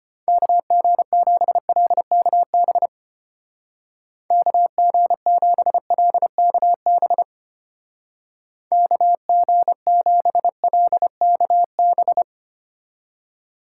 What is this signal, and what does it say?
Morse code, heard as a keyed tone, about 30 wpm, 700 Hz. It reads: KG7LKB KG7LKB KG7LKB
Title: Speed Racing - USA 2x3 Call Signs 25wpm